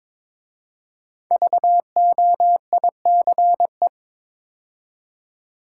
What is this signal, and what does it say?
Morse code, heard as a keyed tone, about 22 wpm, 700 Hz. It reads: VOICE
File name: 5 Letter Words - First 3 Letters Given - 22wpm